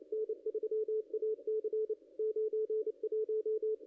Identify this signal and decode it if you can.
Morse code, heard as a keyed tone, about 29 wpm, 440 Hz, with radio noise. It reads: N3AC 91
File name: contests / ss-cw-k5tr-op-n5rz -0500